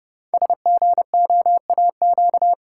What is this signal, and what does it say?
Morse code, heard as a keyed tone, about 30 wpm, 700 Hz. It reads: SGOAQ
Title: Code Groups - 5 Letters - 30wpm